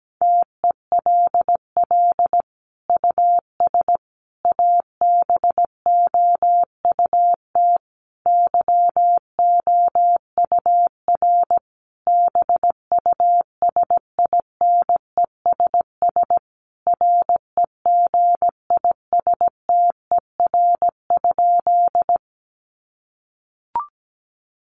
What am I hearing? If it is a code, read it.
Morse code, heard as a keyed tone, about 17 wpm, 700 Hz. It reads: TELL US ABOUT YOUR BUSINESS REGISTER?
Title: Sentences from Top 1200 Words 17wpm